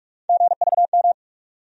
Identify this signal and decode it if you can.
Morse code, heard as a keyed tone, about 46 wpm, 700 Hz. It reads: GVM
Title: USA Call Sign Postfixes - 3 Letters 45wpm